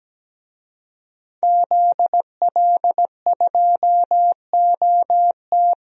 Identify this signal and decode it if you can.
Morse code, heard as a keyed tone, about 17 wpm, 700 Hz. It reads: ZL2OT
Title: DXCC-100 Call Signs - 5 Characters - 17wpm